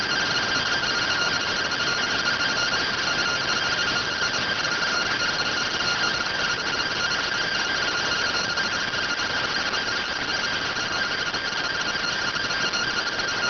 Two-tone teleprinter signal, about 170 Hz shift.